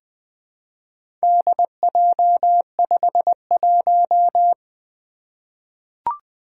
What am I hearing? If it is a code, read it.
Morse code, heard as a keyed tone, about 20 wpm, 700 Hz. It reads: DJ51
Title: Four Letter-Number Combinations - 20wpm